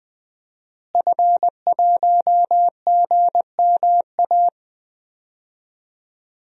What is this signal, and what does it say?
Morse code, heard as a keyed tone, about 20 wpm, 700 Hz. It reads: F1GMA
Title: DXCC-100 Call Signs - 5 Characters - Repeated 3x - 20wpm